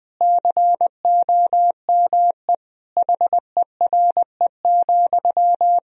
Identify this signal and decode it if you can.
Morse code, heard as a keyed tone, about 21 wpm, 700 Hz. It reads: COME HERE,
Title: Sentences from Top 1000 Words - Review Part 4 - 20wpm